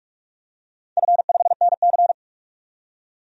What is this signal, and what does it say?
Morse code, heard as a keyed tone, about 46 wpm, 700 Hz. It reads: F5NC
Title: DXCC-100 Call Signs - 4 Characters - 45wpm